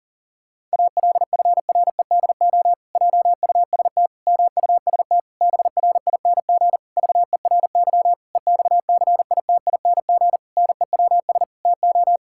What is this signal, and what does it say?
Morse code, heard as a keyed tone, about 40 wpm, 700 Hz. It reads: ALFREDO JUST MUST BRING VERY EXCITING NEWS TO
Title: Pangram Challenge v2 - 40wpm